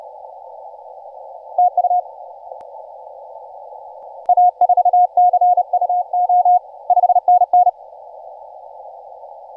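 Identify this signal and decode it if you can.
Morse code, heard as a keyed tone, about 32 wpm, 720 Hz. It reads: TU A4CUO 5NN
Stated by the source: Morse from DX station 5T5OK on 20 m CW, posted by IZ8GNR.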